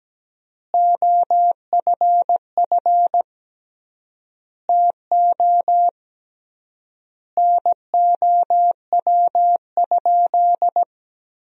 Morse code, heard as a keyed tone, about 17 wpm, 700 Hz. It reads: OFF TO NOW?
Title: 3x Word-Spacing - Sentences from Top 200 Words - 17wpm